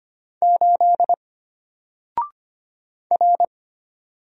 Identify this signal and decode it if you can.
Morse code, heard as a keyed tone, about 25 wpm, 700 Hz. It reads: 8 R
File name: Single-Letter-Number-25wpm